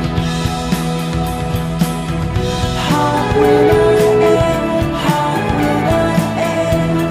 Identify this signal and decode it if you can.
Ringback tone (Europe, 425 Hz), with noise.